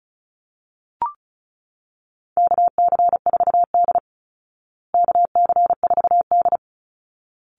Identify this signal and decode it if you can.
Morse code, heard as a keyed tone, about 34 wpm, 700 Hz. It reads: KC4D KC4D
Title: Call Signs Repeated 3x - 35wpm